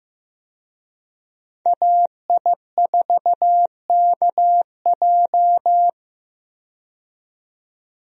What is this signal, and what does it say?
Morse code, heard as a keyed tone, about 15 wpm, 700 Hz. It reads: AI4KJ
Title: USA Call Signs - 5 Characters - 15wpm